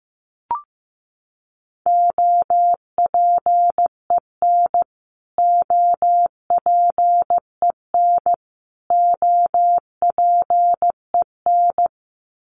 Morse code, heard as a keyed tone, about 15 wpm, 700 Hz. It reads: OPEN OPEN OPEN
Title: Top 300 Words Repeated 15wpm